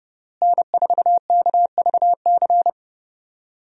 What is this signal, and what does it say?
Morse code, heard as a keyed tone, about 30 wpm, 700 Hz. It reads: N4KVC
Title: USA Call Signs - 5 Characters - Repeated 3x - 30wpm